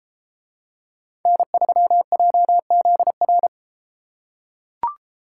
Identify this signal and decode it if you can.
Morse code, heard as a keyed tone, about 32 wpm, 700 Hz. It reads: N3JZR